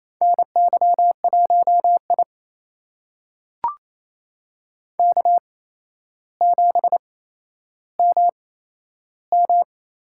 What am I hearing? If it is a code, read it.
Morse code, heard as a keyed tone, about 28 wpm, spaced as for 12 wpm, 700 Hz. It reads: NY1I K7MM
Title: USA Call Signs - 4 Characters - 8x Character Spacing - 28wpm